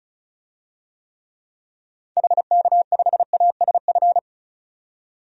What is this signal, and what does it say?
Morse code, heard as a keyed tone, about 36 wpm, 700 Hz. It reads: HK5ASF
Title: DXCC-100 Call Signs - 6 Characters - 35wpm